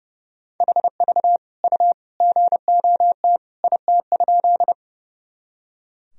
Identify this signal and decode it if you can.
Morse code, heard as a keyed tone, about 30 wpm, 700 Hz. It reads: HV U GOT IT?
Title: Sentences from Top 300 Words and Abbreviations 30wpm